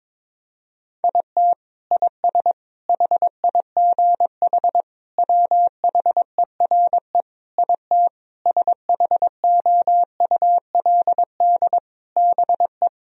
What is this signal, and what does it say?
Morse code, heard as a keyed tone, about 22 wpm, 700 Hz. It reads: IT IS HIGH WHERE IT SHOULD BE
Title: Sentences from Top 1000 Words - Review Part 1 - 22wpm